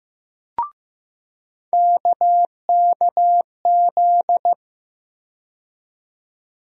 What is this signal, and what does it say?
Morse code, heard as a keyed tone, about 15 wpm, 700 Hz. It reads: KKZ
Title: USA Call Sign Postfixes - 3 Letters 15wpm